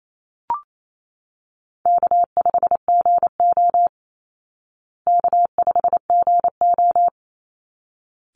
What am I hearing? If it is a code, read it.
Morse code, heard as a keyed tone, about 28 wpm, 700 Hz. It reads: K5GO K5GO
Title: Easy Call Signs Repeated 2x - 28wpm